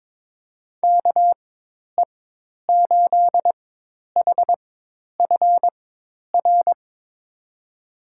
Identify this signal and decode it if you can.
Morse code, heard as a keyed tone, about 22 wpm, spaced as for 10 wpm, 700 Hz. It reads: KE8HFR
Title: USA Call Signs - 6 Characters - 4x Character Spacing - 22wpm